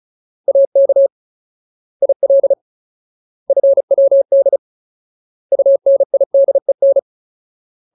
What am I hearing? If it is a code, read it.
Morse code, heard as a keyed tone, about 35 wpm, 550 Hz. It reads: AK IL FWD UNIDEN
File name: Ultimate Word List / 35WPM Ultimate